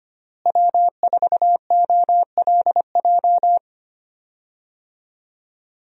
Morse code, heard as a keyed tone, about 25 wpm, 700 Hz. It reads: W4OLJ
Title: USA Call Signs - 5 Characters - Repeated 2x - 25wpm